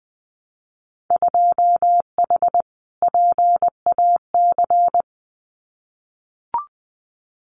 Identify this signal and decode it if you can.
Morse code, heard as a keyed tone, about 20 wpm, 700 Hz. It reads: 2H PAC
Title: Winter Field Day Exchanges Encore 20wpm